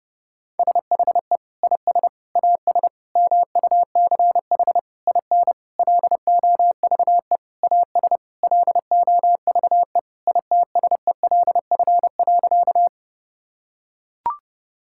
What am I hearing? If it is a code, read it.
Morse code, heard as a keyed tone, about 30 wpm, 700 Hz. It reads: SHE IS AS MUCH IN LOVE AS LOVE ITSELF.
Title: Sentences from Top 1000 Words - Review Part 5 - 30wpm